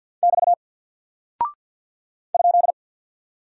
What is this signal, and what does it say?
Morse code, heard as a keyed tone, about 49 wpm, 700 Hz. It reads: <BT> ?
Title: Single Character (Letter:Number:Punctuation:Prosign) Alt 50wpm